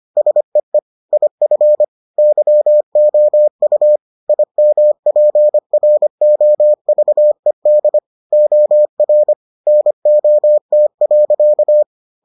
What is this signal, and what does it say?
Morse code, heard as a keyed tone, about 25 wpm, 600 Hz. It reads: SEE IF YOU IMPROVED OR NOT.